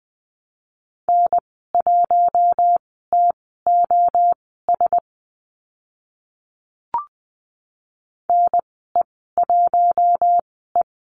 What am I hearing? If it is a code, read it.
Morse code, heard as a keyed tone, about 20 wpm, spaced as for 14 wpm, 700 Hz. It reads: N1TOS NE1E
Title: 042 Lesson 8g - Number 1 - Call Signs 20wpm